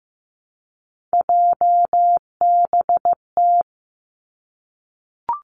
Morse code, heard as a keyed tone, about 15 wpm, 700 Hz. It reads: JBT
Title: Three Challenging Letter-Number Combinations 15wpm